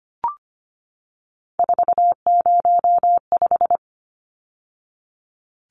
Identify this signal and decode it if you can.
Morse code, heard as a keyed tone, about 25 wpm, 700 Hz. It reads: 405
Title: Numbers - 3-Digit 25wpm